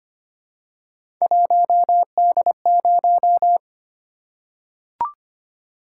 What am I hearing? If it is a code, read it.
Morse code, heard as a keyed tone, about 25 wpm, 700 Hz. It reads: 1D0
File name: Three Letter-Number Combinations - 25wpm